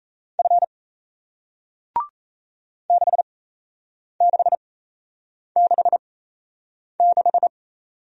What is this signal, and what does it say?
Morse code, heard as a keyed tone, about 37 wpm, spaced as for 8 wpm, 700 Hz. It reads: F 6666